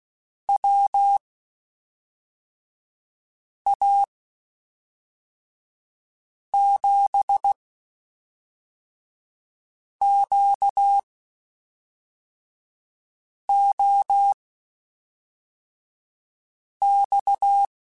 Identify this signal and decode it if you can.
Morse code, heard as a keyed tone, about 16 wpm, spaced as for 3 wpm, 800 Hz. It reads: WA7QOX